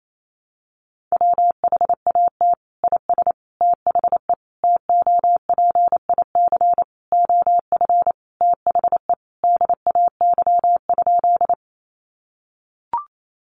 Morse code, heard as a keyed tone, about 28 wpm, 700 Hz. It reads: WHAT IS THE TOPIC OF THE DAY?
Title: Sentences from Top 1200 Words 28wpm